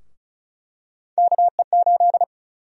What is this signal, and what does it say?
Morse code, heard as a keyed tone, about 36 wpm, 700 Hz. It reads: KE8
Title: USA Call Sign Prefixes - Repeated 2x - 35wpm